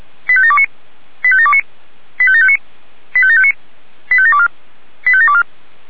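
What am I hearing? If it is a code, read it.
Five-tone selective call (ZVEI), heard as quick runs of five tones, tones of about 70 ms. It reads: 85729 85729 85759 85759 85724 85724
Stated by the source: ZVEI five-tone call 85730 received on an SDR.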